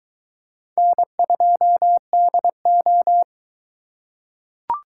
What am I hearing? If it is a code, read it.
Morse code, heard as a keyed tone, about 22 wpm, 700 Hz. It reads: N2DO